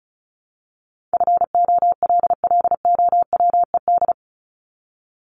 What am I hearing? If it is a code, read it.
Morse code, heard as a keyed tone, about 34 wpm, 700 Hz. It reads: FOLLOWED
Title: ED Suffix - Top 100 Words with Base Words 35wpm